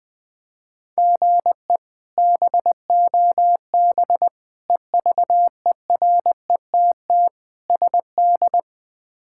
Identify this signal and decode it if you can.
Morse code, heard as a keyed tone, about 20 wpm, 700 Hz. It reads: GE BOB EVERETT SD
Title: K1USN SST QSO Practice - 20wpm